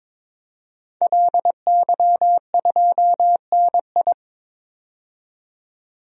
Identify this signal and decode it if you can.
Morse code, heard as a keyed tone, about 22 wpm, 700 Hz. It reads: LY2NI
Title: DXCC-100 Call Signs - 5 Characters - 22wpm